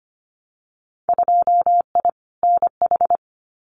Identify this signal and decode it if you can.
Morse code, heard as a keyed tone, about 25 wpm, 690 Hz. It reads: 2I NH
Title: Winter Field Day Exchanges Encore 25wpm